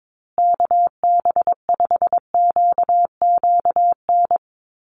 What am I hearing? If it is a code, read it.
Morse code, heard as a keyed tone, about 22 wpm, 700 Hz. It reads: KB5QQN
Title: USA Call Signs Repeated 3x 22wpm